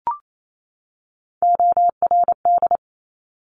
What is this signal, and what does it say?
Morse code, heard as a keyed tone, about 28 wpm, 700 Hz. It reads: ORD
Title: Top Three Letter Combinations in Words 28wpm